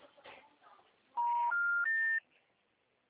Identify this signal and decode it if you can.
Special information tone, with speech.
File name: out- -2019- - - .67702